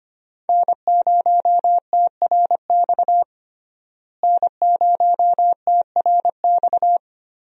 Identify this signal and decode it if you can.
Morse code, heard as a keyed tone, about 25 wpm, 700 Hz. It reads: N0TRX N0TRX
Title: USA Call Signs - 5 Characters - Repeated 2x - 25wpm